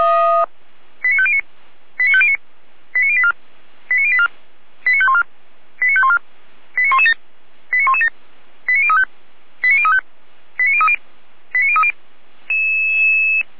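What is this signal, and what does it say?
Five-tone selective call (ZVEI), heard as quick runs of five tones, tones of about 70 ms. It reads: 89509 89509 89084 89084 89514 89514 89117 89117 89035 89035 89030 89030
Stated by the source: Received by SDR.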